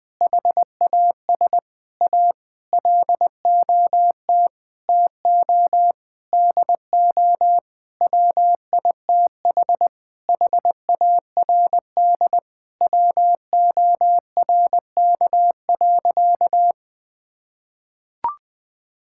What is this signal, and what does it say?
Morse code, heard as a keyed tone, about 21 wpm, 700 Hz. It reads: HAS A LOT TO DO WITH HARD WORK.
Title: Sentences with 6-Word Phrases - 20wpm